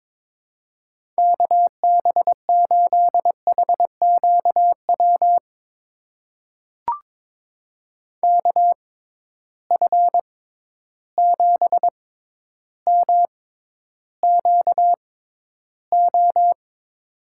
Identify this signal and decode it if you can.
Morse code, heard as a keyed tone, about 22 wpm, spaced as for 11 wpm, 700 Hz. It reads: KB8HQW KF7MQO
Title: USA Call Signs - 6 Characters - 6x Character Spacing - 22wpm